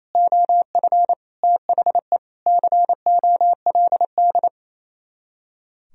Morse code, heard as a keyed tone, about 33 wpm, 700 Hz. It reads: OF THE COLD
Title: Speed-Racing - Sets of 5 Words 28wpm